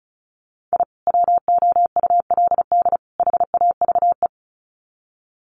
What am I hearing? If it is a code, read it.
Morse code, heard as a keyed tone, about 34 wpm, 700 Hz. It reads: I WOULD HAVE
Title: Sets of 3 Words 35wpm